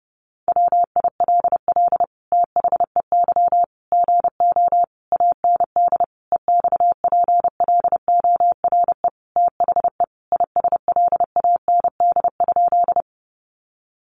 Morse code, heard as a keyed tone, about 30 wpm, 700 Hz. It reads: WILL THEY GO AND EXPLORE THE ISLAND?
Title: Sentences from Top 1100 Words 30wpm